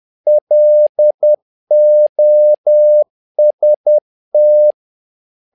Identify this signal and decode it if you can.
Morse code, heard as a keyed tone, about 10 wpm, 600 Hz. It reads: LOST
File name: learning center / tower trouble / tower trouble 10wpm